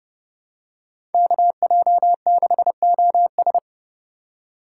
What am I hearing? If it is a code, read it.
Morse code, heard as a keyed tone, about 30 wpm, 700 Hz. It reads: KJ6OS
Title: USA Call Signs - 5 Characters - 2x Character Spacing - 30wpm